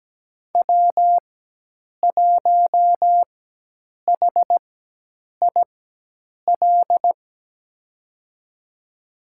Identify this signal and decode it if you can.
Morse code, heard as a keyed tone, about 17 wpm, spaced as for 8 wpm, 700 Hz. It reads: W1HIL
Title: USA Call Signs - 5 Characters - 4x Character Spacing - 17wpm